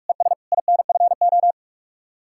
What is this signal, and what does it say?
Morse code, heard as a keyed tone, about 56 wpm, 700 Hz. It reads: ES INFO
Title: Speed-Racing Common QSO Phrases - 45wpm